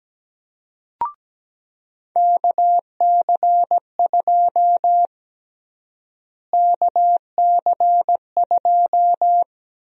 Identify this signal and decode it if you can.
Morse code, heard as a keyed tone, about 17 wpm, 700 Hz. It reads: KC2 KC2
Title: USA Call Sign Prefixes - Repeated 2x - 17wpm